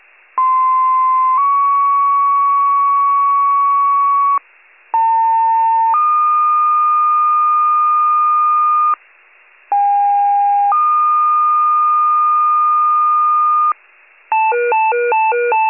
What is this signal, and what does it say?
Two-tone page, heard as a one-second tone, then a three-second tone, about 1040 then 1120 Hz, about 900 then 1220 Hz, about 810 then 1220 Hz. It is heard from 0.4 to 13.7 s.
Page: 95 RESCUE